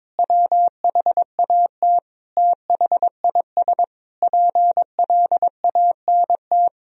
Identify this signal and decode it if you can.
Morse code, heard as a keyed tone, about 21 wpm, 700 Hz. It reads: WHAT THIS PLANT
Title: Sentences from Top 1000 Words - Review Part 1 - 22wpm